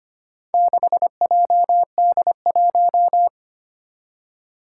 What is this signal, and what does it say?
Morse code, heard as a keyed tone, about 25 wpm, 700 Hz. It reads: 6JD1
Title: Four Challenging Letter-Number Combinations 25wpm